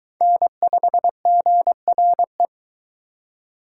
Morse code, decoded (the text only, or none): N5GRE